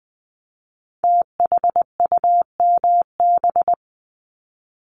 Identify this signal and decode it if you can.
Morse code, heard as a keyed tone, about 20 wpm, 700 Hz. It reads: THUMB
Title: Multi-letter Phonemes Repeated 3x 20wpm